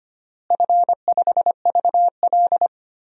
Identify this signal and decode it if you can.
Morse code, heard as a keyed tone, about 25 wpm, 700 Hz. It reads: F5VL
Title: DXCC-100 Call Signs - 4 Characters - Repeated 3x - 25wpm